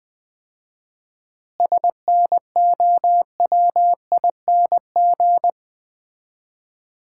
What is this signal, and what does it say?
Morse code, heard as a keyed tone, about 20 wpm, 700 Hz. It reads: SNOWING
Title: Common QSO Phrases - 4x Word-Spacing - 20wpm